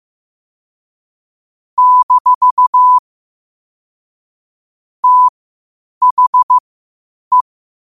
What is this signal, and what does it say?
Morse code, heard as a keyed tone, about 15 wpm, spaced as for 8 wpm, 1000 Hz. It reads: - THE
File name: qotdfiles / 8wpm